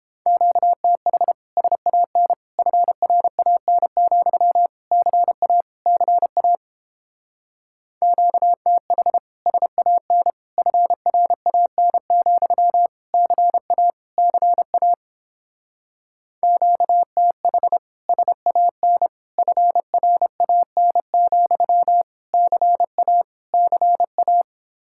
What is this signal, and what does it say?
Morse code, heard as a keyed tone, about 29 wpm, 700 Hz. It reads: QTH SAN FRAN, CA CA QTH SAN FRAN, CA CA QTH SAN FRAN, CA CA